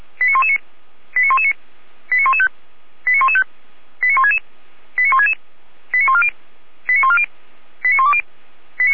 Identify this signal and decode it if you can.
Five-tone selective call (ZVEI), heard as quick runs of five tones, tones of about 70 ms. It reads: 89119 89119 89115 89115 89166 89166 89140 89140 89120
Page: /455/SDR/ZVEI/89072/2022.01.19/